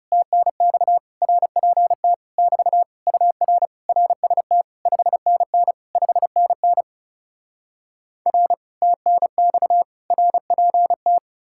Morse code, heard as a keyed tone, about 40 wpm, 700 Hz. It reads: TNX RPT <BT> UR RST 5NN 5NN R TNX RPT